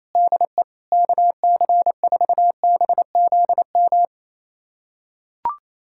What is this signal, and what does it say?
Morse code, heard as a keyed tone, about 28 wpm, 700 Hz. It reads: DE KC4BZM
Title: USA Callsign DE Callsign - 6 Characters - 28wpm